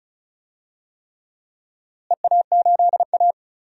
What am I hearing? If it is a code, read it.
Morse code, heard as a keyed tone, about 36 wpm, 700 Hz. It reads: EA8A
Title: DXCC-100 Call Signs - 4 Characters - Repeated 3x - 35wpm